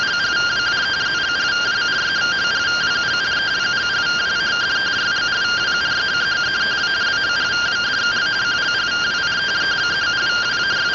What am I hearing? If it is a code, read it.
Two-tone teleprinter signal, about 170 Hz shift.